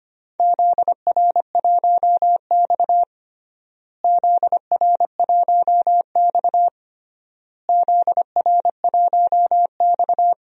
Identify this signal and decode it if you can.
Morse code, heard as a keyed tone, about 25 wpm, 700 Hz. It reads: ZR1X ZR1X ZR1X
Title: DXCC-100 Call Signs - 4 Characters - Repeated 3x - 25wpm